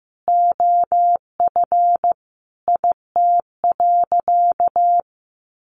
Morse code, heard as a keyed tone, about 15 wpm, 700 Hz. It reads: OF IT.
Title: Sentences from Top 1400 Words 15wpm